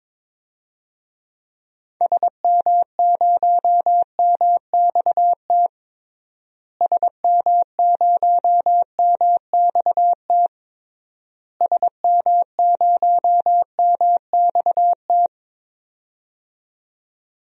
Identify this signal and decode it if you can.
Morse code, heard as a keyed tone, about 22 wpm, 700 Hz. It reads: SM0MXT SM0MXT SM0MXT
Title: DXCC-100 Call Signs - 6 Characters - Repeated 3x - 22wpm